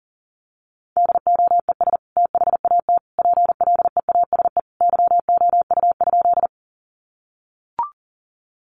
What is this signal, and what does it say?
Morse code, heard as a keyed tone, about 40 wpm, 700 Hz. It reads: DOES THAT PLEASE YOU?
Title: Sentences from Top 1500 Words 40wpm